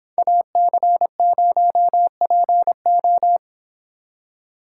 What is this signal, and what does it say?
Morse code, heard as a keyed tone, about 29 wpm, 700 Hz. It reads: AC0PO